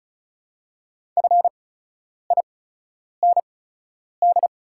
Morse code, heard as a keyed tone, about 36 wpm, spaced as for 10 wpm, 700 Hz. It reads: FIND
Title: 8x Character Spacing - Top 100 Words - 35wpm